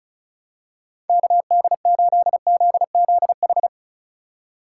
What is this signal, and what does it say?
Morse code, heard as a keyed tone, about 36 wpm, 700 Hz. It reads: KD8ZZH
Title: USA Call Signs - 6 Characters - 2x Character Spacing - 35wpm